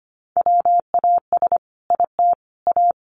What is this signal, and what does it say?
Morse code, heard as a keyed tone, about 25 wpm, 700 Hz. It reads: WAS IT A
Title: Sentences from Top 1100 Words 25wpm